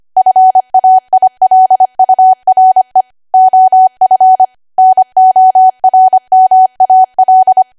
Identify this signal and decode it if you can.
Morse code, heard as a keyed tone, about 25 wpm, 750 Hz. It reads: FAILURE OF NORMAL
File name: Archive / 25 WPM / 25WPM